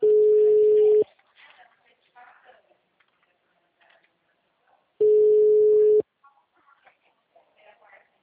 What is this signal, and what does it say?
Ringback tone (Europe, 425 Hz), with speech.
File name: out- -2019- - - .57847